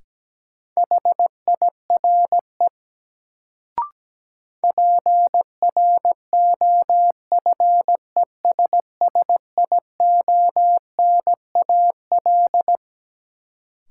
Morse code, heard as a keyed tone, about 17 wpm, 700 Hz. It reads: HIRE PROFESSIONAL
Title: Sentences from Top 1400 Words - New Word Drill Alt - 17wpm